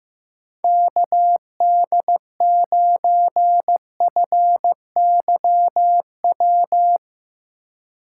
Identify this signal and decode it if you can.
Morse code, heard as a keyed tone, about 15 wpm, 700 Hz. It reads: KD9FYW
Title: USA Call Signs - 6 Characters - 3x Character Spacing - 15wpm